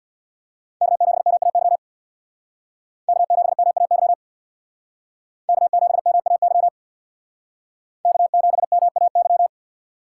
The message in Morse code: K7MAY K7MAY K7MAY K7MAY

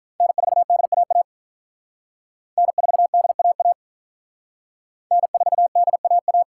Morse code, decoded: N4DAA N4DAA N4DAA

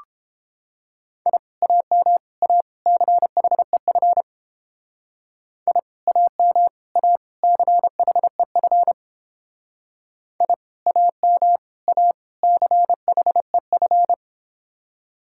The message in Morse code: I AM A CHEF I AM A CHEF I AM A CHEF

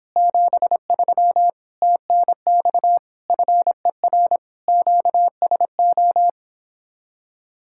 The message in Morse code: 73 TNX FER QSO

